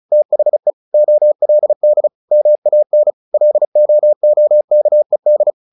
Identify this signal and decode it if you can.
Morse code, heard as a keyed tone, about 34 wpm, 600 Hz. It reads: THE OLD MAN LOOKED
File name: the old man and the sea / the old man and the sea 1 35wpm